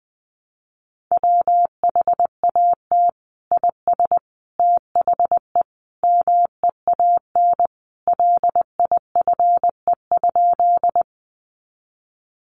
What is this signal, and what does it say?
Morse code, heard as a keyed tone, about 20 wpm, 700 Hz. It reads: WHAT IS THE MEAN LIFE?
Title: Sentences from Top 900 Words 20wpm